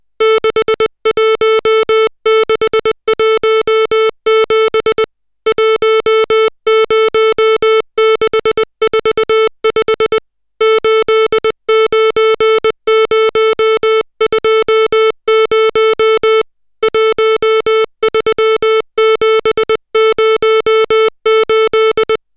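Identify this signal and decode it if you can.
Morse code, heard as a keyed tone, about 20 wpm, 440 Hz, with noise. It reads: 61617 10645 89020 13708
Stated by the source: Morse code drill of digits at 60 characters per minute.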